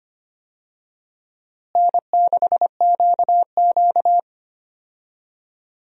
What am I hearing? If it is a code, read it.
Morse code, heard as a keyed tone, about 25 wpm, 700 Hz. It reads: N6QQ
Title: USA Call Signs - 4 Characters - Repeated 2x - 25wpm